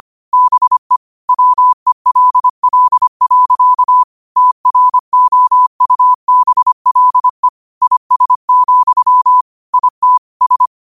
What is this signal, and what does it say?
Morse code, heard as a keyed tone, about 25 wpm, 1000 Hz. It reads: BE WELL. TROUBLE IS, IT S
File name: qotdfiles / 25wpm